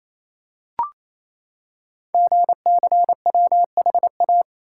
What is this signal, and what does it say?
Morse code, heard as a keyed tone, about 27 wpm, 700 Hz. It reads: GCWHA
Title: Code Groups - 5 Letter:Numbers - 28wpm